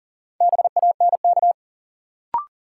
Morse code, heard as a keyed tone, about 40 wpm, 700 Hz. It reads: BANK
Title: Sets of 5 Words - New Word Drill - 40wpm